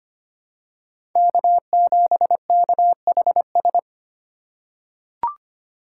Morse code, heard as a keyed tone, about 25 wpm, 700 Hz. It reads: K7KHS
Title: USA Call Signs - 5 Characters - 2x Character Spacing - 25wpm